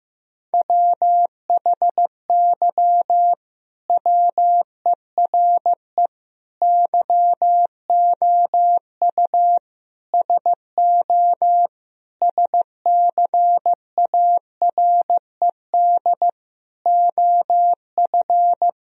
Morse code, heard as a keyed tone, about 15 wpm, 700 Hz. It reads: WHY WERE YOU SO SCARED OF
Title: Sentences from Top 1000 Words - Review Part 4 - 15wpm